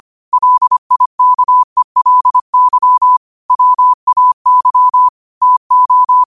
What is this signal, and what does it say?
Morse code, heard as a keyed tone, about 25 wpm, 1000 Hz. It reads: LIKELY WAY TO
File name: qotdfiles / 25wpm